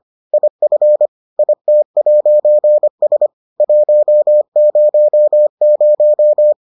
Morse code, heard as a keyed tone, about 25 wpm, 600 Hz. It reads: IF IT'S 100